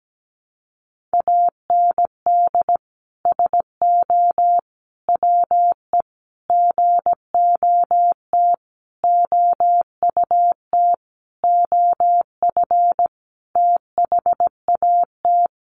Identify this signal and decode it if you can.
Morse code, heard as a keyed tone, about 17 wpm, 700 Hz. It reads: AND SO WE GOT OUT OF THAT
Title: Sentences from Top 1500 Words 17wpm